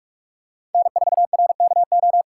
Morse code, heard as a keyed tone, about 42 wpm, 700 Hz. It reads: N4RKO
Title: USA Call Signs - 5 Characters - 4x Character Spacing - 45wpm